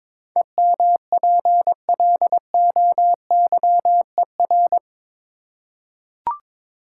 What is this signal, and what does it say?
Morse code, heard as a keyed tone, about 22 wpm, 700 Hz. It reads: EMPLOYER
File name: practice / 22wpm / Eight Letter Words - 22wpm